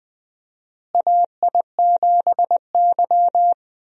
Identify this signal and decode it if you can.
Morse code, heard as a keyed tone, about 20 wpm, 700 Hz. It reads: AI7Y
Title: USA Call Signs - 4 Characters - Repeated 3x - 20wpm